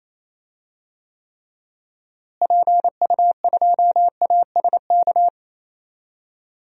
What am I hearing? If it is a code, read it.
Morse code, heard as a keyed tone, about 28 wpm, 700 Hz. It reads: PU2ASK
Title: DXCC-100 Call Signs - 6 Characters - Repeated 3x - 28wpm